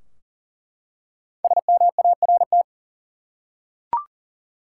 Morse code, heard as a keyed tone, about 39 wpm, 700 Hz. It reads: SMART
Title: Five Letter Words - 40wpm